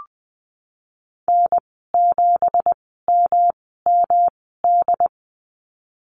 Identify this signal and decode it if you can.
Morse code, heard as a keyed tone, about 20 wpm, spaced as for 15 wpm, 700 Hz. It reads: N7MMD
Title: 186 Lesson 30g - Letter b - Call Signs 20wpm